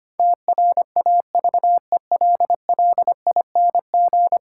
Morse code, heard as a keyed tone, about 25 wpm, 700 Hz. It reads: TRAVELLING
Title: Pangram Challenge v2 - 25wpm